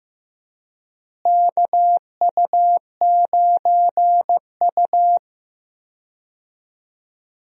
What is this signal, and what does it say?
Morse code, heard as a keyed tone, about 15 wpm, 700 Hz. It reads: KU9U
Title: USA Call Signs - 4 Characters - Repeated 3x - 15wpm